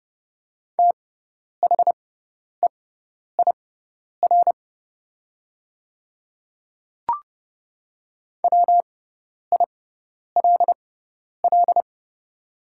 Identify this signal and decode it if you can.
Morse code, heard as a keyed tone, about 31 wpm, spaced as for 10 wpm, 700 Hz. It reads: THEIR WILL
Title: 6x Character Spacing - Top 100 Words - 30wpm